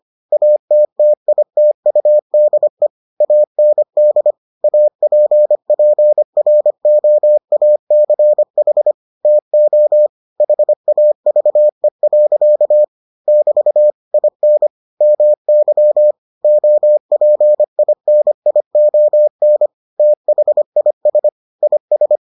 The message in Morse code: ATTITUDE AND APPROACH TO HAVE. <BT> IN MY OPINION THIS IS

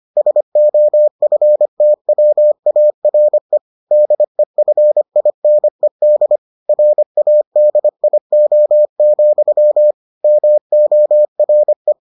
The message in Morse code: SOFTWARE DEFINED RADIO, MORE

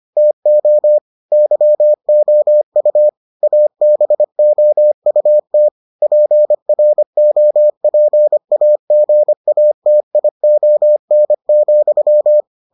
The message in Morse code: TO YOU ABOUT PROPAGATION,

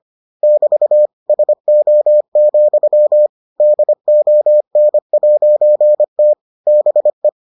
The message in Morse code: <BT> SO, DON'T BE

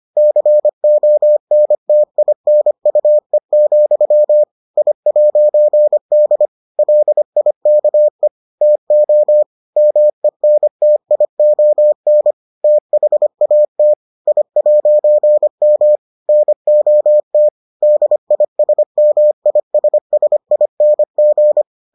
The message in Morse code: CONTINUE, I'D LIKE TO MENTION THAT I'M NOT DISMISSING

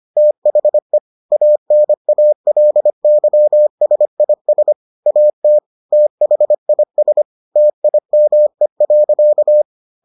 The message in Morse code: THE ANALYSIS AT THIS TIME.

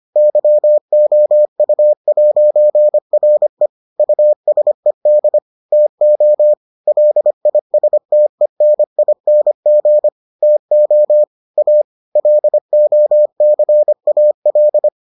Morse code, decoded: YOU'RE USED TO LISTENING TO A LOCAL

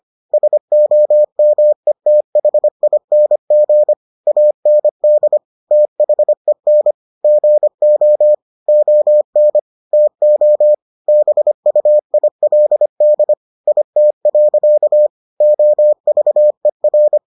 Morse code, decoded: SOMETHING AND THEN GO ON TO BUILD IT. OVER

